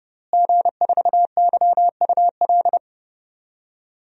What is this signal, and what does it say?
Morse code, heard as a keyed tone, about 30 wpm, 700 Hz. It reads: G4YUL
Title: DXCC-100 Call Signs - 5 Characters - Repeated 3x - 30wpm